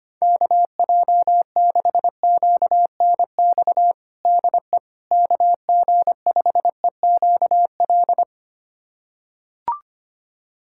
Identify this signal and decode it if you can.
Morse code, heard as a keyed tone, about 25 wpm, 700 Hz. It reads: KJ6QNX DE KG5EQL
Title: USA Callsign DE Callsign - 6 Characters - 25wpm